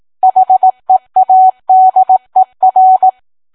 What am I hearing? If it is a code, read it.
Morse code, heard as a keyed tone, about 18 wpm, 750 Hz. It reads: HEADER